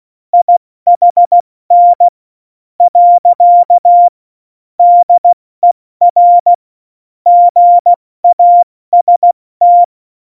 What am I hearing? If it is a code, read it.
Morse code, heard as a keyed tone, about 16 wpm, spaced as for 13 wpm, 700 Hz. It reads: IHN. DER GAST
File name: Morsekurs / Witze / Witz 16. April 2019, 70 F80